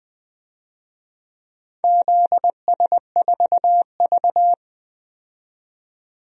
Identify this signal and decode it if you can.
Morse code, heard as a keyed tone, about 20 wpm, 700 Hz. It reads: ZS4V
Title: DXCC-100 Call Signs - 4 Characters - 20wpm